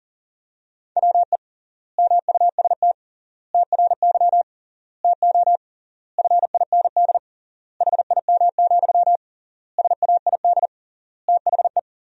Morse code, heard as a keyed tone, about 40 wpm, 700 Hz. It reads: WE MUST TRY TO FIND HIM, SAID THE
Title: 3x Word-Spacing - Sentences from Top 400 Words - 40wpm